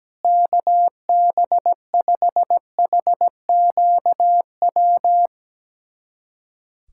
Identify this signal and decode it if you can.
Morse code, heard as a keyed tone, about 17 wpm, 700 Hz. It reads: KB5HQW
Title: USA Callsign DE Callsign 17wpm